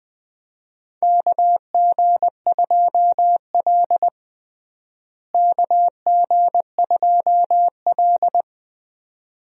Morse code, heard as a keyed tone, about 20 wpm, 700 Hz. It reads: KG2L KG2L
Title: USA Call Signs - 4 Characters - Repeated 2x - 20wpm